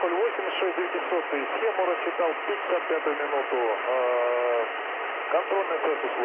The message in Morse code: none

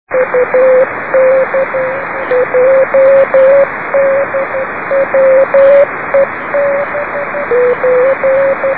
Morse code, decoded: none

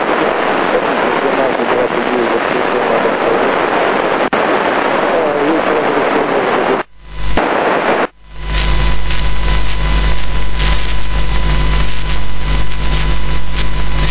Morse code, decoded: none

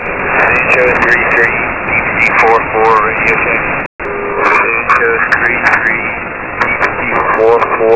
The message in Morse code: none

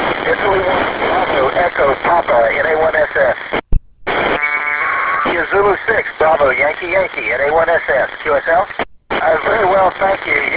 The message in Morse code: none

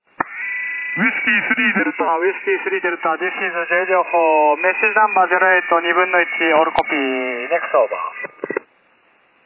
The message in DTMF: none

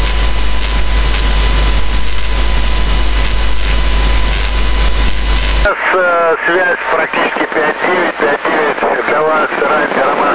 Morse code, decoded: none

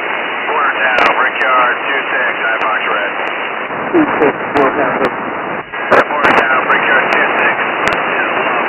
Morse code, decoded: none

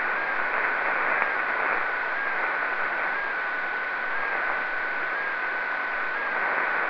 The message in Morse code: none

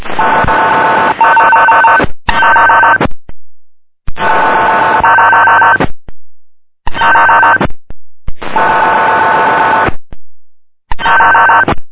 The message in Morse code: none

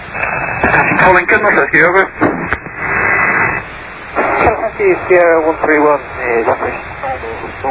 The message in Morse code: none

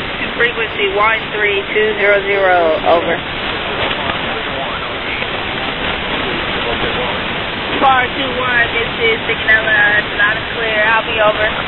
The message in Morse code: none

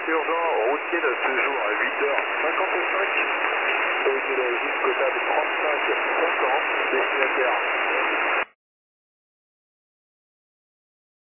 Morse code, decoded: none

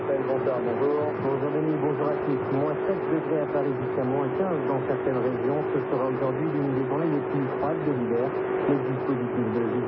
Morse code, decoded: none

